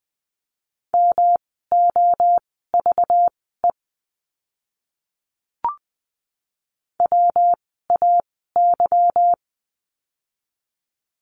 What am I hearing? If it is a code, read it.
Morse code, heard as a keyed tone, about 20 wpm, spaced as for 15 wpm, 700 Hz. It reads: MOVE WAY
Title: 237 Lesson 38f - Letter q - Words 20wpm